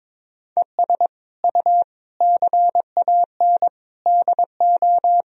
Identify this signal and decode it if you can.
Morse code, heard as a keyed tone, about 22 wpm, 700 Hz. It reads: ES U CAN DO
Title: Sentences from Top 500 Words and Abbreviations 22wpm